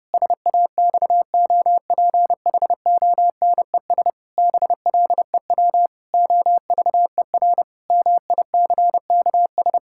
Morse code, heard as a keyed tone, about 30 wpm, 700 Hz. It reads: SAXOPHONES BLEW OVER MICKS
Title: Pangram Challenge v2 - 30wpm